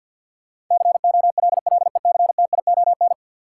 Morse code, heard as a keyed tone, about 47 wpm, 700 Hz. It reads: COLLECTION
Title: Sentences from Top 1200 Words - New Word Drill - 50wpm